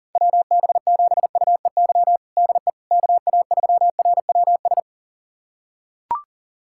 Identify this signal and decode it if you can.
Morse code, heard as a keyed tone, about 37 wpm, 700 Hz. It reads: WB7UEY DE KA3RWS
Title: USA Callsign DE Callsign - 6 Characters - 40wpm